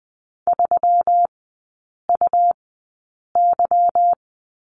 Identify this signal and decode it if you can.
Morse code, heard as a keyed tone, about 20 wpm, spaced as for 8 wpm, 700 Hz. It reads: 3UY
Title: 184 Lesson 30e - Letter b - Three Characters 20wpm